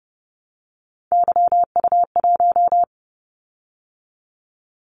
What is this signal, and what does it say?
Morse code, heard as a keyed tone, about 30 wpm, 700 Hz. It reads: YU1
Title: Callsign Prefixes for Top 25 DXCC - 30wpm